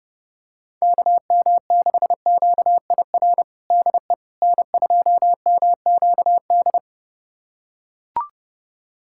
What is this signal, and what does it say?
Morse code, heard as a keyed tone, about 30 wpm, 700 Hz. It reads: KM6QIR DE N2MQD
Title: USA Callsign DE Callsign 30wpm